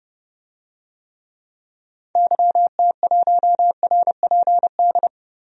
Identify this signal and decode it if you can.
Morse code, heard as a keyed tone, about 30 wpm, 700 Hz. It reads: YT1RPD
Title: DXCC-100 Call Signs - 6 Characters - 30wpm